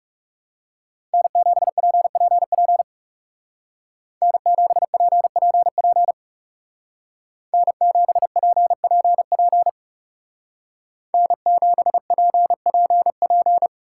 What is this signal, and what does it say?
Morse code, heard as a keyed tone, about 37 wpm, 700 Hz. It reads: N7PPP N7PPP N7PPP N7PPP
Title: Speed Racing - USA 1x3 Call Signs 30wpm